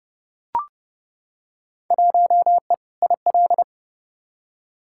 Morse code, heard as a keyed tone, about 32 wpm, 700 Hz. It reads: 1E IL
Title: Field Day Exchanges - 30wpm